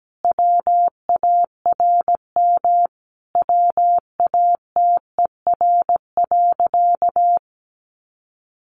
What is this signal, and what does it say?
Morse code, heard as a keyed tone, about 17 wpm, 700 Hz. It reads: WARM WATER.
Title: A Sound Of Thunder 17wpm